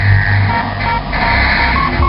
There is no code.